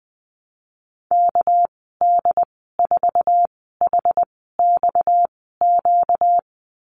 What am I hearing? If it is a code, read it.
Morse code, heard as a keyed tone, about 20 wpm, spaced as for 15 wpm, 700 Hz. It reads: KD4HXQ
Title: 245 Lesson 39g - Letter z - Call Signs 20wpm